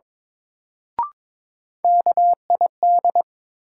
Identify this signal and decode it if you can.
Morse code, heard as a keyed tone, about 22 wpm, 700 Hz. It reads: KID
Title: Sets of 5 Words - New Word Drill - 22wpm